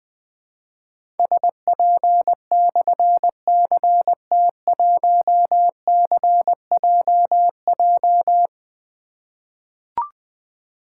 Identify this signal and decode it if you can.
Morse code, heard as a keyed tone, about 20 wpm, 700 Hz. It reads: SP/CT1CJJ
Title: Difficult Call Signs 20wpm